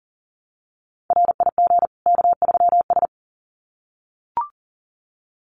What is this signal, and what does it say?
Morse code, heard as a keyed tone, about 40 wpm, 700 Hz. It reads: RIG K3S
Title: Common QSO Phrases 40wpm